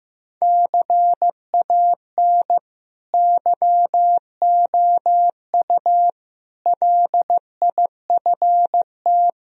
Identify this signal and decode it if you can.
Morse code, heard as a keyed tone, about 15 wpm, 700 Hz. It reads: CAN YOU LIFT
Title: Sentences from Top 1000 Words - Review Part 3 - 15wpm